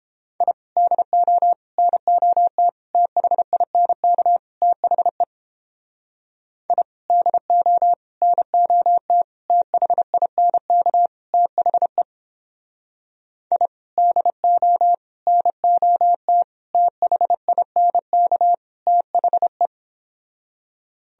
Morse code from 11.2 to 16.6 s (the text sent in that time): THE I DO NOT